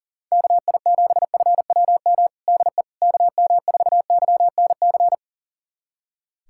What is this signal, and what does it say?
Morse code, heard as a keyed tone, about 40 wpm, 700 Hz. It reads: KI7FWM DE KM4YNC
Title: USA Callsign DE Callsign - 6 Characters - 40wpm